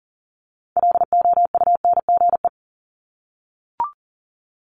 Morse code, heard as a keyed tone, about 41 wpm, 700 Hz. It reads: LOUNGE
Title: Multi-letter Phonemes 40wpm